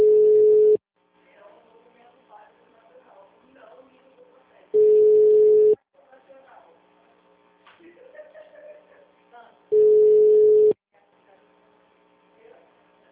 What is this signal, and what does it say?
Ringback tone (Europe, 425 Hz), with speech.